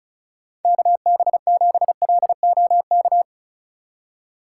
Morse code, heard as a keyed tone, about 36 wpm, 700 Hz. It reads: KB7LOK
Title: USA Call Signs - 6 Characters - 4x Character Spacing - 35wpm